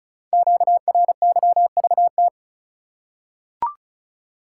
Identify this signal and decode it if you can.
Morse code, heard as a keyed tone, about 36 wpm, 700 Hz. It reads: QRYVT
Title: Code Groups - 5 Letter:Numbers - 35wpm